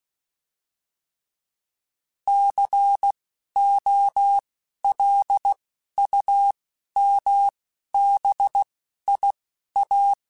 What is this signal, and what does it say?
Morse code, heard as a keyed tone, about 16 wpm, spaced as for 11 wpm, 800 Hz. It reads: COLUMBIA
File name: fromSMs / FromSMs mp3s / Cities@10